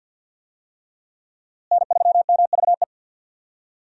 Morse code, heard as a keyed tone, about 53 wpm, 700 Hz. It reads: N3MVE
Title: USA Call Signs - 5 Characters - Repeated 3x - 50wpm